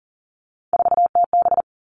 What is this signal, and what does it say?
Morse code, heard as a keyed tone, about 40 wpm, 700 Hz. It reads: 4TB
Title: All Cut Numbers - 3-Digit 40wpm